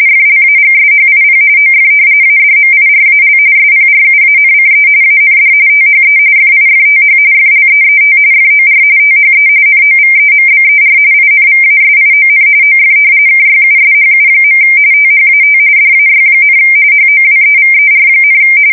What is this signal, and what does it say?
RTTY, about 45.45 baud. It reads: FLAGSHIP OF ADMIRAL SPRUANCE AND THE FIFTH FLEET FOR THE NOVEMBER INVASION OF KYUSHU. THAT DAY, HOWEVER,
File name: fsk170 / battleshipnj